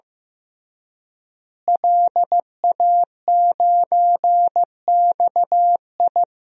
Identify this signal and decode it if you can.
Morse code, heard as a keyed tone, about 15 wpm, 700 Hz. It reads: LA9XI
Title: DXCC-100 Call Signs - 5 Characters - Repeated 3x - 15wpm